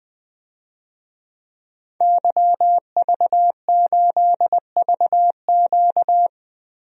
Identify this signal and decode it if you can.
Morse code, heard as a keyed tone, about 20 wpm, 700 Hz. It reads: YV8VQ
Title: DXCC-100 Call Signs - 5 Characters - Repeated 2x - 20wpm